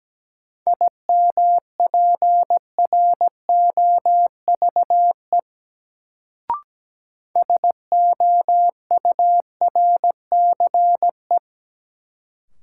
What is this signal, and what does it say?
Morse code, heard as a keyed tone, about 17 wpm, 700 Hz. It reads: IMPROVE SOURCE
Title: Sentences from Top 800 Words - New Word Drill - 17wpm